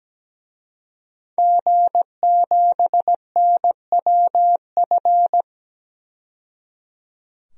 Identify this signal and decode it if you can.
Morse code, heard as a keyed tone, about 17 wpm, 700 Hz. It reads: G7NWF
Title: DXCC-100 Call Signs - 5 Characters - 17wpm